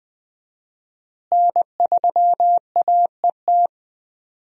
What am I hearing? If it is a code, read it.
Morse code, heard as a keyed tone, about 20 wpm, 700 Hz. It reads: N3AET
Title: USA Call Signs - 5 Characters - Repeated 2x - 20wpm